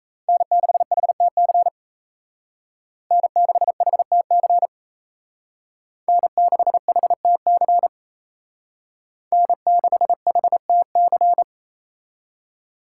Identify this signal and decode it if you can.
Morse code, heard as a keyed tone, about 33 wpm, 700 Hz. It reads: N6HTC N6HTC N6HTC N6HTC
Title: Speed Racing - USA 1x3 Call Signs 28wpm